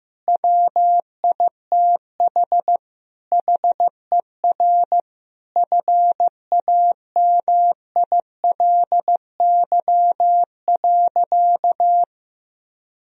Morse code, heard as a keyed tone, about 15 wpm, 700 Hz. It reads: WITH HER FAMILY.
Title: Sentences from Top 1000 Words - Review Part 3 - 15wpm